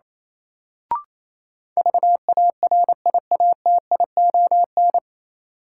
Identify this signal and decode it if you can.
Morse code, heard as a keyed tone, about 28 wpm, 700 Hz. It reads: VARIATION
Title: Sentences from Top 1600 Words - New Word Drill - 28wpm